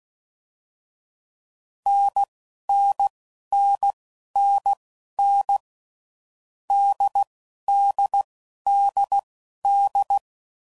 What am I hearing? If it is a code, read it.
Morse code, heard as a keyed tone, about 16 wpm, spaced as for 11 wpm, 800 Hz. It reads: NNNNN DDDD
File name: Combinations / Difficult Combos / NDB NDB6